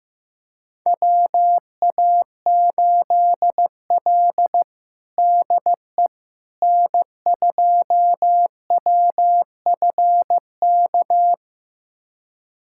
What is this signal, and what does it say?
Morse code, heard as a keyed tone, about 16 wpm, 700 Hz. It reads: WA8L DE N2WFK
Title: USA Callsign DE Callsign 15wpm